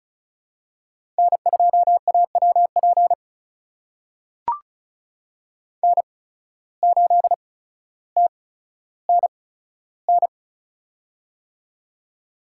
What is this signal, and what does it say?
Morse code, heard as a keyed tone, about 34 wpm, spaced as for 15 wpm, 700 Hz. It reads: N2AWP N8TNN
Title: USA Call Signs - 5 Characters - 8x Character Spacing - 35wpm